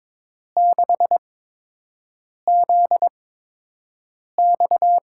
Morse code, heard as a keyed tone, about 22 wpm, spaced as for 6 wpm, 700 Hz. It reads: 6ZX
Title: USA Call Signs - 6 Characters - 8x Character Spacing - 22wpm